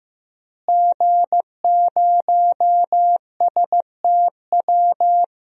Morse code, heard as a keyed tone, about 15 wpm, 700 Hz. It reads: G0STW
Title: DXCC-100 Call Signs - 5 Characters - Repeated 3x - 15wpm